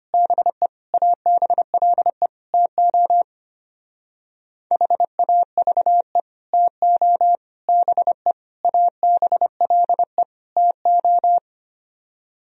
Morse code, heard as a keyed tone, about 31 wpm, 700 Hz. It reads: BE ABLE TO HAVE TO BE ABLE TO
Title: Speed-Racing - Sets of 5 Words 25wpm